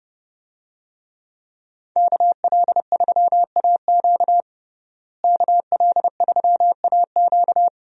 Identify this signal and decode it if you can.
Morse code, heard as a keyed tone, about 30 wpm, 700 Hz. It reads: KL3AQ KL3AQ
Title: USA Call Signs - 5 Characters - Repeated 3x - 30wpm